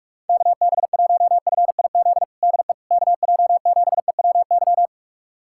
Morse code, heard as a keyed tone, about 42 wpm, 700 Hz. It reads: KB1FIZ DE KJ7EWY
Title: USA Callsign DE Callsign 45wpm